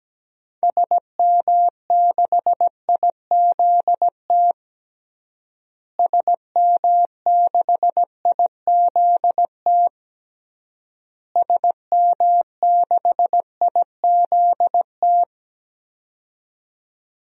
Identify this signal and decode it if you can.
Morse code, heard as a keyed tone, about 17 wpm, 700 Hz. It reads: SM6IZT SM6IZT SM6IZT
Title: DXCC-100 Call Signs - 6 Characters - Repeated 3x - 17wpm